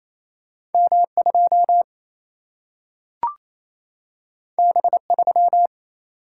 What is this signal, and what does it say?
Morse code, heard as a keyed tone, about 28 wpm, 700 Hz. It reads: M 2 B 3
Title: Single-Letter Single-Number 28wpm